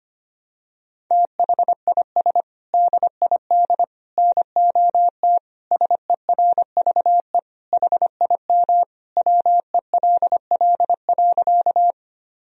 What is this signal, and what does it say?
Morse code, heard as a keyed tone, about 25 wpm, 700 Hz. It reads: THIS DID NOT SERVE HIM WELL.
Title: Sentences from Top 1000 Words - Review Part 2 - 25wpm